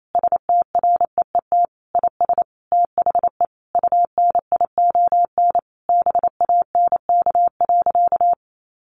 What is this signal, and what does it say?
Morse code, heard as a keyed tone, about 28 wpm, 700 Hz. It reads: STREET IS THE UNION BANK.
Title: Sentences from Top 1400 Words 28wpm